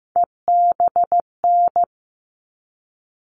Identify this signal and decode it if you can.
Morse code, heard as a keyed tone, about 15 wpm, 700 Hz. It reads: EBN
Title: All Cut Numbers - 3-Digit 15wpm